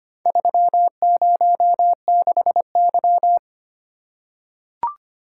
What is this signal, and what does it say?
Morse code, heard as a keyed tone, about 25 wpm, 700 Hz. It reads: 306Y
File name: practice / 25wpm / Four Letter-Number Combinations - 25wpm